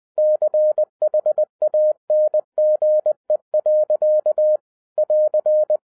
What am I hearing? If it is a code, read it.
Morse code, heard as a keyed tone, about 20 wpm, 610 Hz. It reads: CHANGE. <AR>
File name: 2011-07-mp3 / n08 A LOT HAS CHANGED 20wpm